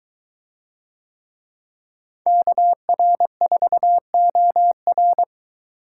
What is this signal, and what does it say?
Morse code, heard as a keyed tone, about 22 wpm, 700 Hz. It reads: KR4OR